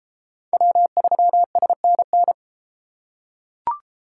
Morse code, heard as a keyed tone, about 33 wpm, 700 Hz. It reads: W3SNN